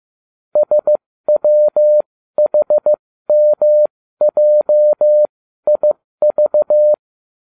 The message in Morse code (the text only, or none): SWHMJIV